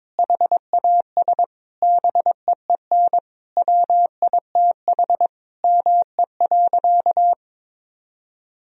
Morse code, heard as a keyed tone, about 22 wpm, 700 Hz. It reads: HAS BEEN WITH ME.
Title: Sentences from Top 1000 Words - Review Part 1 - 22wpm